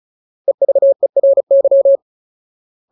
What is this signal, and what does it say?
Morse code, heard as a keyed tone, about 34 wpm, 550 Hz. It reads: EVERY
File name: Ultimate Word List / 35WPM Ultimate